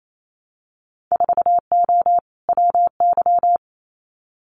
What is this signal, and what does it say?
Morse code, heard as a keyed tone, about 28 wpm, 700 Hz. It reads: 4O WY
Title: Winter Field Day Exchanges Encore 28wpm